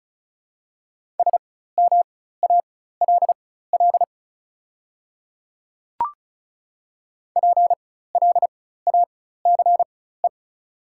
Morse code, heard as a keyed tone, about 34 wpm, spaced as for 16 wpm, 690 Hz. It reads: SMALL PLACE
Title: 4x Character Spacing - Top 200 Words - 35wpm